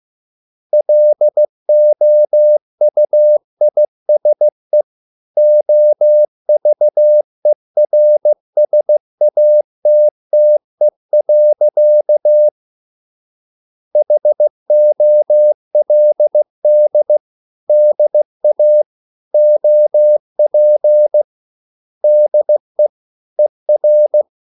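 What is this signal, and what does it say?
Morse code, heard as a keyed tone, about 16 wpm, 600 Hz. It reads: LOUISE OVERSATTE. HOLD DA OP DE ER
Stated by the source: Morse code at a slow 15 wpm from CW med Gnister.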